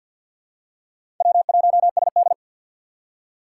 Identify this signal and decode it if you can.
Morse code, heard as a keyed tone, about 48 wpm, 700 Hz. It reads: W1SD
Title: USA Call Signs - 6x Character Spacing - 50wpm